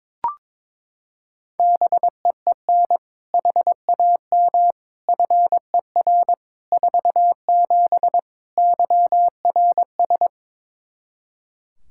Morse code, heard as a keyed tone, about 22 wpm, 700 Hz. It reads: BEEN HAM FER 47 YRS
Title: Practice QSO - Exchange 3 - More Info - 22wpm